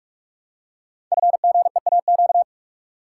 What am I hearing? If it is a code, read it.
Morse code, heard as a keyed tone, about 44 wpm, 700 Hz. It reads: FGEAQ
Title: Code Groups - 5 Letters - 45wpm